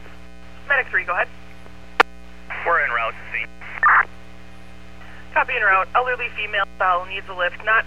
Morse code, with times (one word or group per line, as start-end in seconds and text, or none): none